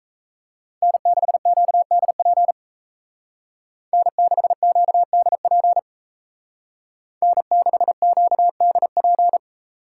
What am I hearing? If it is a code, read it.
Morse code, heard as a keyed tone, about 35 wpm, 700 Hz. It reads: N6QDP N6QDP N6QDP
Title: Speed Racing - USA 1x3 Call Signs 28wpm